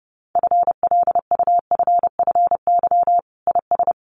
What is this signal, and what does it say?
Morse code, heard as a keyed tone, about 30 wpm, 700 Hz. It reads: FLUFFY IS
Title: Easy Phrases 30wpm